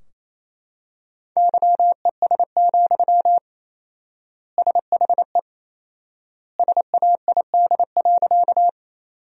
Morse code, heard as a keyed tone, about 27 wpm, 700 Hz. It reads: YES, SHE SAID.
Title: 4x Word-Spacing - Sentences from Top 500 Words - 28wpm